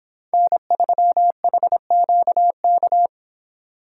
Morse code, heard as a keyed tone, about 29 wpm, 700 Hz. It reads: N3HQK